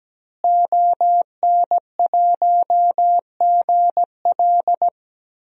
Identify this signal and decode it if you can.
Morse code, heard as a keyed tone, about 17 wpm, 700 Hz. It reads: ON1GL
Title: DXCC-100 Call Signs - 5 Characters - Repeated 2x - 17wpm